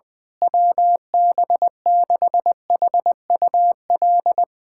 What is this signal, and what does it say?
Morse code, heard as a keyed tone, about 20 wpm, 700 Hz. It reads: WB6HUL
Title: USA Callsign DE Callsign 20wpm